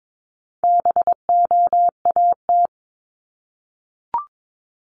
Morse code, heard as a keyed tone, about 22 wpm, 700 Hz. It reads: BOAT
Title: Multi-letter Phonemes Repeated 3x 22wpm